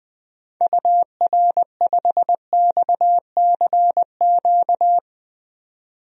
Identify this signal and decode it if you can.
Morse code, heard as a keyed tone, about 20 wpm, 700 Hz. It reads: UR5XCQ
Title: DXCC-100 Call Signs - 6 Characters - 20wpm